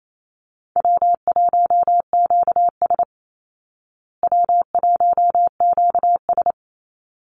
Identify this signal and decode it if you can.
Morse code, heard as a keyed tone, about 28 wpm, 700 Hz. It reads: W1QS W1QS
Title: Easy Call Signs Repeated 2x - 28wpm